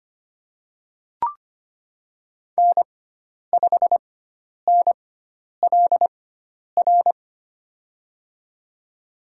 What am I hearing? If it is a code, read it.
Morse code, heard as a keyed tone, about 25 wpm, spaced as for 10 wpm, 700 Hz. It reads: N5NLR
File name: Lesson 16g - Letter c - Call Signs 25wpm